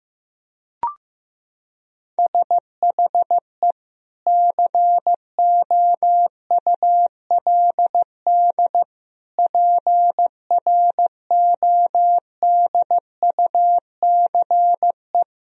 Morse code, heard as a keyed tone, about 16 wpm, 700 Hz. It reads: SHE COULD PRODUCE
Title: Sentences from Top 1000 Words - Review Part 3 - 15wpm